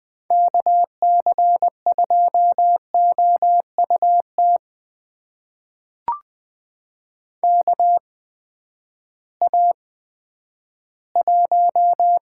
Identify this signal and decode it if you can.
Morse code, heard as a keyed tone, about 20 wpm, spaced as for 11 wpm, 700 Hz. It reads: KC2OUT KA1
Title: USA Call Signs - 6 Characters - 8x Character Spacing - 20wpm